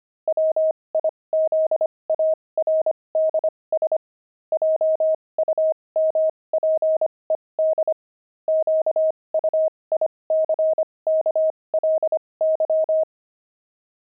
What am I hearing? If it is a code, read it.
Morse code, heard as a keyed tone, about 25 wpm, spaced as for 20 wpm, 630 Hz. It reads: WIZARDS JUMPED QUICKLY
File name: pangram-fs-20-cs-25-jumping